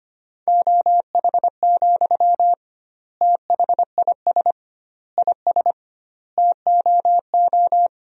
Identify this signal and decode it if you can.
Morse code, heard as a keyed tone, about 25 wpm, 700 Hz. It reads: OH, THIS IS TOO
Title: 2x Word-Spacing - Sentences from Top 500 Words - 25wpm